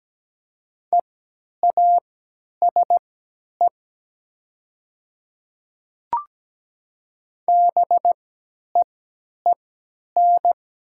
Morse code, heard as a keyed tone, about 17 wpm, spaced as for 10 wpm, 700 Hz. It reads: EASE BEEN
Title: 3x Character Spacing - Top 500 Words - 17wpm